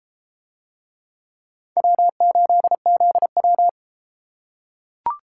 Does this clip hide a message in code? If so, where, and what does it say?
Yes: Morse code from 1.8 to 3.7 s, heard as a keyed tone, about 32 wpm, 700 Hz. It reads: W8ZW